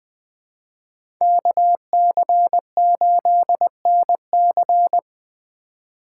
Morse code, heard as a keyed tone, about 20 wpm, 700 Hz. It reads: KC8NC